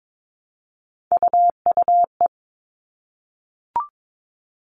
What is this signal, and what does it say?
Morse code, heard as a keyed tone, about 22 wpm, 700 Hz. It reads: UUE
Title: All Cut Numbers - 3-Digit 22wpm